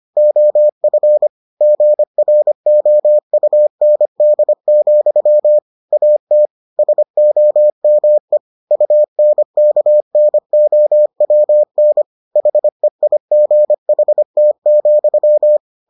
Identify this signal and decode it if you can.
Morse code, heard as a keyed tone, about 25 wpm, 600 Hz. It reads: OF GROUND, AT SOME UNKNOWN HEIGHT,